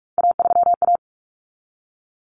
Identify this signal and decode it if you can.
Morse code, heard as a keyed tone, about 44 wpm, 700 Hz. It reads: A3A
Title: Common Cut Numbers - 3-Digit 45wpm